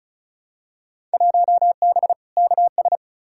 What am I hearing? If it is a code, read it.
Morse code, heard as a keyed tone, about 36 wpm, 700 Hz. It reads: 1B KS
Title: Field Day Exchanges Encore - 35wpm